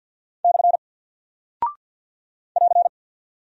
Morse code, heard as a keyed tone, about 48 wpm, 700 Hz. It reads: / <AR>
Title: Single Character (Letter:Number:Punctuation:Prosign) Alt 50wpm